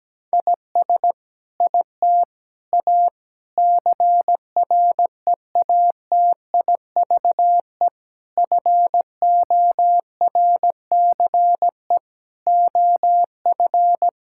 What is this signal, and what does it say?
Morse code, heard as a keyed tone, about 17 wpm, 700 Hz. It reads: IS IT A CREATIVE FORCE OF
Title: Sentences from Top 1000 Words - Review Part 5 - 17wpm